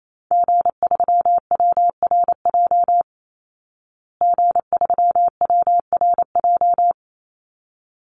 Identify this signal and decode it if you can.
Morse code, heard as a keyed tone, about 28 wpm, 700 Hz. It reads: G3WRJ G3WRJ
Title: Call Signs Repeated 3x - 28wpm